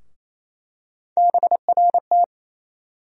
Morse code, heard as a keyed tone, about 28 wpm, 700 Hz. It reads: BRT
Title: USA Call Sign Postfixes - 3 Letters 28wpm